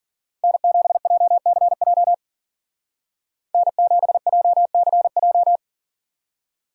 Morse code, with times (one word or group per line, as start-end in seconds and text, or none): 0.4-2.1 N7JCJ
3.5-5.6 N7JCJ